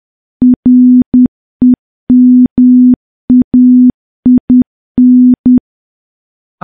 Morse code, heard as a keyed tone, about 10 wpm, 250 Hz. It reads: REMAIN
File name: morse / morse-training-10WPM